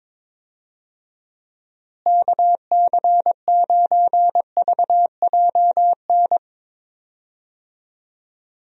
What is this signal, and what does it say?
Morse code, heard as a keyed tone, about 22 wpm, 700 Hz. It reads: KC9VJN
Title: USA Call Signs - 6 Characters - Repeated 2x - 22wpm